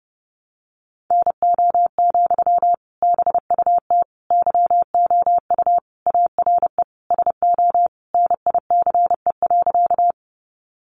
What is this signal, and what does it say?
Morse code, heard as a keyed tone, about 28 wpm, 700 Hz. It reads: NO, BUT YOU ARE SO NICE.
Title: Sentences from Top 1100 Words 30wpm